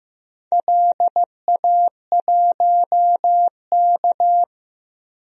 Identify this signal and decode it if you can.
Morse code, heard as a keyed tone, about 15 wpm, 700 Hz. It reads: LA1K
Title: DXCC-100 Call Signs - 4 Characters - 15wpm